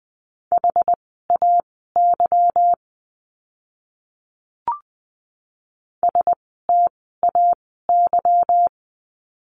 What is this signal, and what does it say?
Morse code, heard as a keyed tone, about 20 wpm, spaced as for 14 wpm, 700 Hz. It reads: HAY STAY
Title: 146 Lesson 24f - Letter y - Words 20wpm